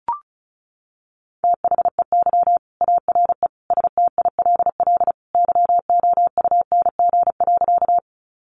Morse code, heard as a keyed tone, about 34 wpm, 700 Hz. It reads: THEY ARE STILL YOUNG.
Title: Sentences from Top 400 Words 35wpm